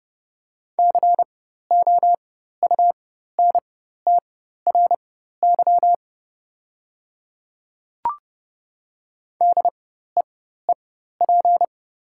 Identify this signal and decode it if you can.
Morse code, heard as a keyed tone, about 30 wpm, spaced as for 14 wpm, 700 Hz. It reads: COUNTRY DEEP
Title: 4x Character Spacing - Top 500 Words - 30wpm